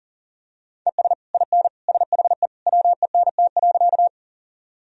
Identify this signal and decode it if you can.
Morse code, heard as a keyed tone, about 40 wpm, 700 Hz. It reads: ES IN SHE WENT.
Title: Sentences from Top 200 Words and Abbreviations 40wpm